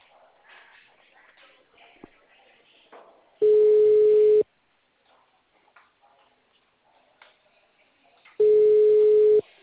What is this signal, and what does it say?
Ringback tone (Europe, 425 Hz).
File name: out- -2021- - - .38